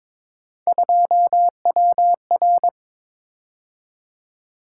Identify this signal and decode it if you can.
Morse code, heard as a keyed tone, about 22 wpm, 700 Hz. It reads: 2WR
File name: practice / 22wpm / Three Letter-Number Combinations - 22wpm